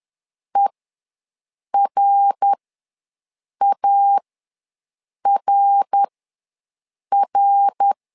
Morse code, heard as a keyed tone, about 11 wpm, spaced as for 6 wpm, 800 Hz. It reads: ERARR
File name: LEZIONI / 3 MWRO / 3 MWR ws 10wpm 2151